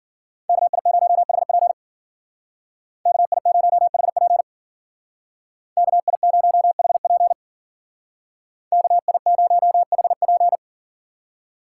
Morse code, heard as a keyed tone, about 52 wpm, 700 Hz. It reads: KI0HP KI0HP KI0HP KI0HP